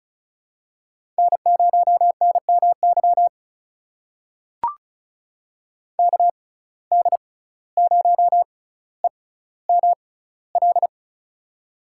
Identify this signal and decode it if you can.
Morse code, heard as a keyed tone, about 36 wpm, spaced as for 17 wpm, 700 Hz. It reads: N0NMY KD0EML
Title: USA Call Signs - 6x Character Spacing - 35wpm